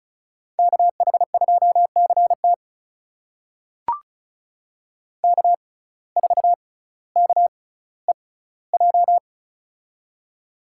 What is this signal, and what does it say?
Morse code, heard as a keyed tone, about 36 wpm, spaced as for 18 wpm, 700 Hz. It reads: KH2CT K4KEJ
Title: USA Call Signs - 5 Characters - 6x Character Spacing - 35wpm